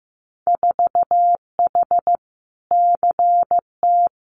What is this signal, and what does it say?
Morse code, heard as a keyed tone, about 15 wpm, 700 Hz. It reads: 4H CT
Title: Winter Field Day Exchanges Encore 15wpm